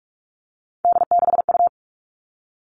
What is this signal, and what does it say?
Morse code, heard as a keyed tone, about 44 wpm, 700 Hz. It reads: D6U
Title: All Cut Numbers - 3-Digit 45wpm